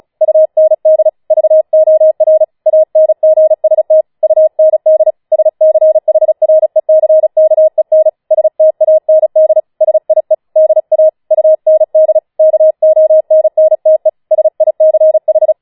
Morse code, heard as a keyed tone, about 36 wpm, 630 Hz. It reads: UND VOR ANGST UND SCHRECKEN STAND SIE DA UND KONNTE SICH
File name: Schneewittchen Teil6 36wpm